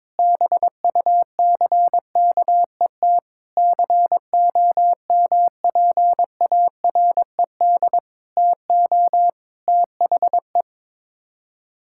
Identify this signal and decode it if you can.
Morse code, heard as a keyed tone, about 22 wpm, 700 Hz. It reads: BUCKET COMPARED TO THE
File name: practice / 22wpm / Sets of 8 Words - 22wpm